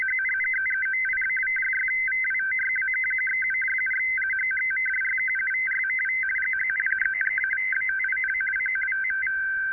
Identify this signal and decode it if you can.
Two-tone teleprinter signal, about 500 Hz shift.
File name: uploader / 5145 ssb